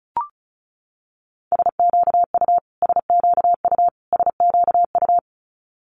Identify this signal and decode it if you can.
Morse code, heard as a keyed tone, about 36 wpm, 700 Hz. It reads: SQU SQU SQU
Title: Multi-letter Phonemes Repeated 3x 35wpm